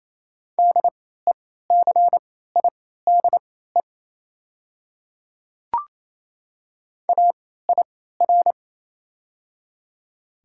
Morse code, heard as a keyed tone, about 28 wpm, spaced as for 16 wpm, 690 Hz. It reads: DECIDE AIR
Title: 3x Character Spacing - Top 500 Words - 28wpm